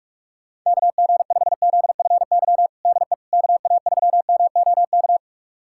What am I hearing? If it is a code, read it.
Morse code, heard as a keyed tone, about 46 wpm, 700 Hz. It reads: KG5ZFY DE KA3MOK
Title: USA Callsign DE Callsign - 6 Characters - 45wpm